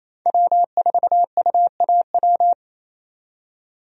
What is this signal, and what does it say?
Morse code, heard as a keyed tone, about 28 wpm, 700 Hz. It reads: W4UAW
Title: USA Call Signs - 5 Characters - 28wpm